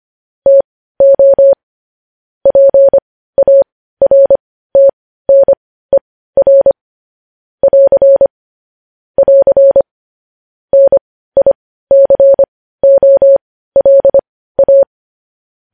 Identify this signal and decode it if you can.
Morse code, heard as a keyed tone, about 25 wpm, spaced as for 15 wpm, 550 Hz. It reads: TO PARTNER <AR> <AR> NICOLA
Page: news-headlines-in-morse-code-at-15-wpm